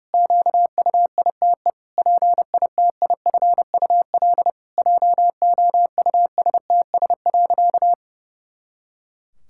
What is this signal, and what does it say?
Morse code, heard as a keyed tone, about 32 wpm, 700 Hz. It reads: QUITE PITIFUL JOUSTS.
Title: Pangram Challenge v2 - 30wpm